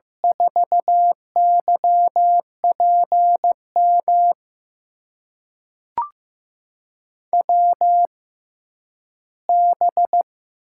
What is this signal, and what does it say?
Morse code, heard as a keyed tone, about 15 wpm, spaced as for 10 wpm, 700 Hz. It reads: 4YPM WB
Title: USA Call Signs - 6 Characters - 6x Character Spacing - 15wpm